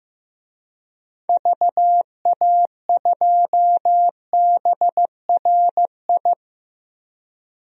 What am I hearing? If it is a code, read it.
Morse code, heard as a keyed tone, about 15 wpm, 700 Hz. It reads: VA2BRI
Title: DXCC-100 Call Signs - 6 Characters - 15wpm